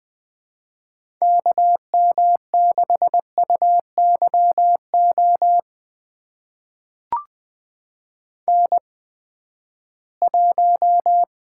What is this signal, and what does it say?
Morse code, heard as a keyed tone, about 21 wpm, spaced as for 14 wpm, 700 Hz. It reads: KM6UYO N1
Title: USA Call Signs - 8x Character Spacing - 20wpm